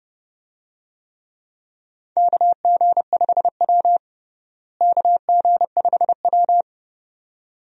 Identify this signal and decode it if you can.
Morse code, heard as a keyed tone, about 30 wpm, 700 Hz. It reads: KG5W KG5W
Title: USA Call Signs - 4 Characters - Repeated 2x - 30wpm